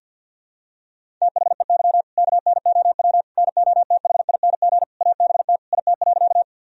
Morse code, heard as a keyed tone, about 53 wpm, 700 Hz. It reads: THEY KNOW NOTHING ABT IT.
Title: Sentences from Top 500 Words and Abbreviations 50wpm